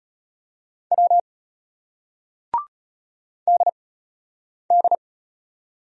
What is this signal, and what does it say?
Morse code, heard as a keyed tone, about 34 wpm, spaced as for 8 wpm, 700 Hz. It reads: W DD